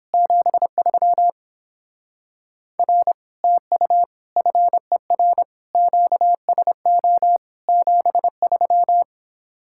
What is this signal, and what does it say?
Morse code, heard as a keyed tone, about 31 wpm, 700 Hz. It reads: 73 R TU FER QSO 73